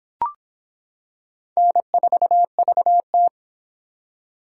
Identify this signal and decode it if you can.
Morse code, heard as a keyed tone, about 26 wpm, 700 Hz. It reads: N4VT